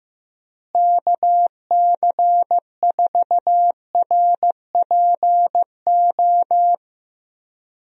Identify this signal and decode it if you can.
Morse code, heard as a keyed tone, about 16 wpm, 700 Hz. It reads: KC4RPO
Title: USA Call Signs - 6 Characters - Repeated 3x - 15wpm